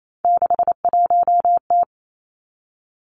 Morse code, heard as a keyed tone, about 28 wpm, 700 Hz. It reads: 61T
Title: Three Challenging Letter-Number Combinations 28wpm